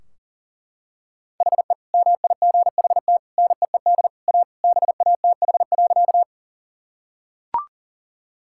Morse code, heard as a keyed tone, about 40 wpm, 700 Hz. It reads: HE MIGHT NEED A BATH.
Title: Sentences from Top 1600 Words 40wpm